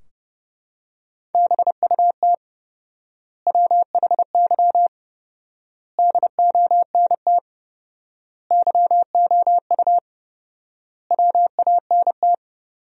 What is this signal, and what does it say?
Morse code, heard as a keyed tone, about 30 wpm, 700 Hz. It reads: BUT WHY DONT YOU WANT
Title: 4x Word-Spacing - Sentences from Top 300 Words - 30wpm